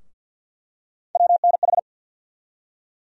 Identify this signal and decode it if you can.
Morse code, heard as a keyed tone, about 51 wpm, 700 Hz. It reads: WNH
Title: USA Call Sign Postfixes - 3 Letters 50wpm